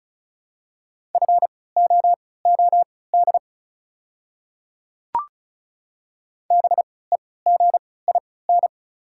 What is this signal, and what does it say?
Morse code, heard as a keyed tone, about 34 wpm, spaced as for 20 wpm, 700 Hz. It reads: FOOD BEGIN
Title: 3x Character Spacing - Top 300 Words - 35wpm